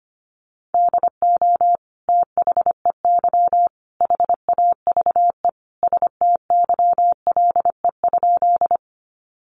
Morse code, heard as a keyed tone, about 25 wpm, 700 Hz. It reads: DO THEY HAVE STYLE?
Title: Sentences from Top 800 Words 25wpm